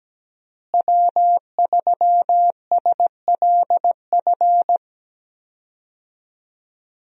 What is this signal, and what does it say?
Morse code, heard as a keyed tone, about 17 wpm, 700 Hz. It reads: W3SLF
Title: USA Call Signs - 5 Characters - Repeated 3x - 17wpm